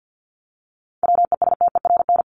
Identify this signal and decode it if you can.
Morse code, heard as a keyed tone, about 50 wpm, 700 Hz. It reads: WESTERN
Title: Top 751-1000 Words 50wpm